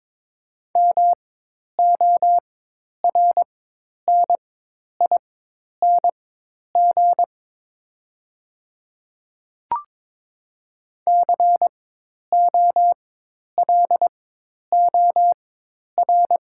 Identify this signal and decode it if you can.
Morse code, heard as a keyed tone, about 22 wpm, spaced as for 10 wpm, 690 Hz. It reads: MORNING COLOR
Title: 4x Character Spacing - Top 500 Words - 22wpm